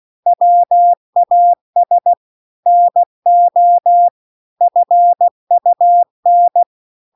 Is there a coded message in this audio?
Yes: Morse code from 0.3 to 6.6 s, heard as a keyed tone, about 16 wpm, 700 Hz. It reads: WAS NO FUN